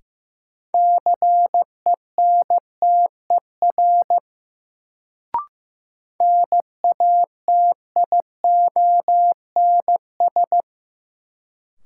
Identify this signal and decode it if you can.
Morse code, heard as a keyed tone, about 16 wpm, 700 Hz. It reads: CENTER NATIONS
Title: Sets of 3 Words - New Word Drill - 15wpm